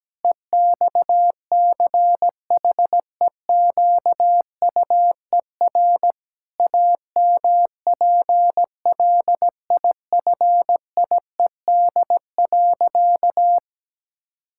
Morse code, heard as a keyed tone, about 17 wpm, 700 Hz. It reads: EXCHEQUER AMPLIFIED.
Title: Pangram Challenge v2 - 17wpm